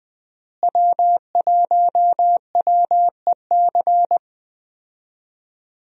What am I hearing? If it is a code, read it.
Morse code, heard as a keyed tone, about 20 wpm, 700 Hz. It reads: W1WEC
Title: USA Call Signs - 5 Characters - Repeated 3x - 20wpm